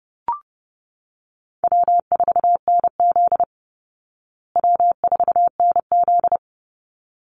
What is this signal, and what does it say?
Morse code, heard as a keyed tone, about 30 wpm, 700 Hz. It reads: W4NZ W4NZ
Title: Call Signs Repeated 3x - 30wpm